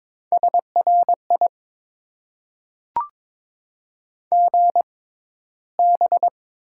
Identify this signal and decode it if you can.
Morse code, heard as a keyed tone, about 22 wpm, spaced as for 13 wpm, 700 Hz. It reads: SRI GB
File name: Top 90 QSO Elements - 6x Character Spacing - 22wpm